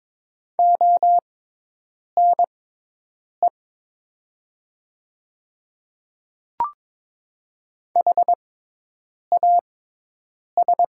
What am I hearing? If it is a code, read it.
Morse code, heard as a keyed tone, about 23 wpm, spaced as for 8 wpm, 700 Hz. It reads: ONE HAS
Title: 6x Character Spacing - Top 100 Words - 22wpm